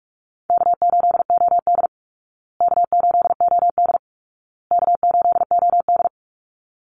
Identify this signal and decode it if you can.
Morse code, heard as a keyed tone, about 44 wpm, 700 Hz. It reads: K8OD K8OD K8OD
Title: Easy Call Signs Repeated 3x - 45wpm